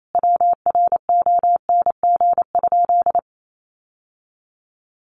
Morse code, heard as a keyed tone, about 28 wpm, 690 Hz. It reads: WRONG?
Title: Sentences from Top 900 Words 28wpm